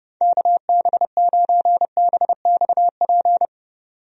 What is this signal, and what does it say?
Morse code, heard as a keyed tone, about 30 wpm, 700 Hz. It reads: KB9BXP
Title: USA Call Signs - 6 Characters - 30wpm